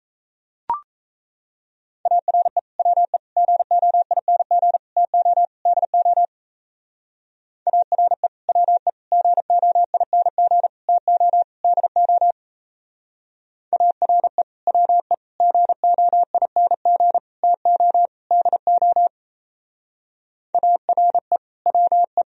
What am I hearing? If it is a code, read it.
Morse code, heard as a keyed tone, about 36 wpm, 700 Hz. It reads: ARE WE GOING TO DO ARE WE GOING TO DO ARE WE GOING TO DO ARE WE